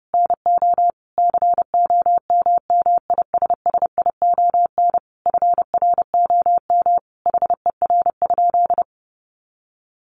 Morse code, heard as a keyed tone, about 30 wpm, 700 Hz. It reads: NO COMMISSION FROM HER?
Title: Sentences from Top 1400 Words 30wpm